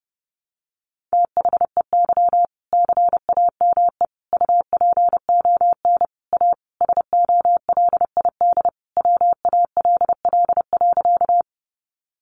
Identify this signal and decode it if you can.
Morse code, heard as a keyed tone, about 30 wpm, 700 Hz. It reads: THEY CAME UPON A SOLID WALL.
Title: Sentences from Top 1500 Words 30wpm